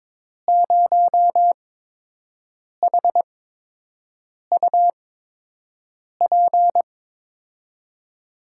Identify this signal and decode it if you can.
Morse code, heard as a keyed tone, about 22 wpm, spaced as for 6 wpm, 700 Hz. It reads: 0HUP
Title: USA Call Signs - 5 Characters - 8x Character Spacing - 22wpm